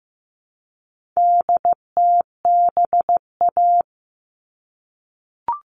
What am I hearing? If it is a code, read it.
Morse code, heard as a keyed tone, about 15 wpm, 700 Hz. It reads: DTBA
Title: All Cut Numbers - 4-Digit 15wpm